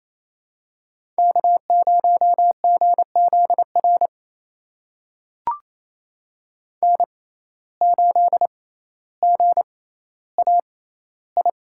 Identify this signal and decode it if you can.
Morse code, heard as a keyed tone, about 28 wpm, spaced as for 14 wpm, 700 Hz. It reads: K0GZR N8GAI
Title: USA Call Signs - 5 Characters - 6x Character Spacing - 28wpm